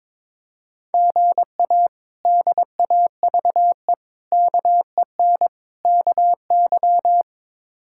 Morse code, heard as a keyed tone, about 22 wpm, 700 Hz. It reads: GA DAVE KEN KY
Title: K1USN SST QSO Practice - 22wpm